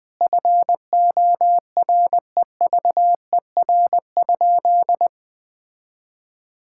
Morse code, heard as a keyed tone, about 21 wpm, 700 Hz. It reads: FOREVER?
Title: Sentences from Top 1600 Words 20wpm